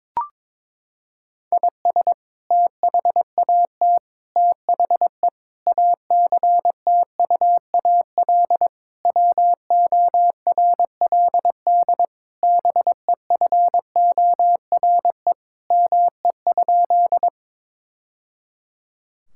Morse code, heard as a keyed tone, about 21 wpm, 700 Hz. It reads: IS THAT THE ACTUAL WORLD BEFORE ME?
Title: Sentences from Top 1000 Words - Review Part 5 - 22wpm